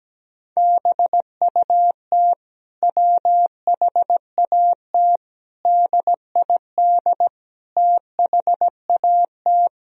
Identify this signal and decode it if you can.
Morse code, heard as a keyed tone, about 17 wpm, 700 Hz. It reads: BUT WHAT DID THAT
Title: Sentences from Top 1000 Words - Review Part 4 - 17wpm